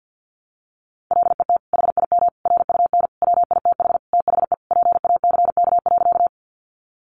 Morse code, heard as a keyed tone, about 50 wpm, 700 Hz. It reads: LET HIM RUN WITH THE PACK.
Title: Sentences from Top 1300 Words 50wpm